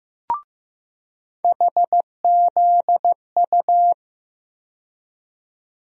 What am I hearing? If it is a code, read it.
Morse code, heard as a keyed tone, about 15 wpm, 700 Hz. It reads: HZU
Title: USA Call Sign Postfixes - 3 Letters 15wpm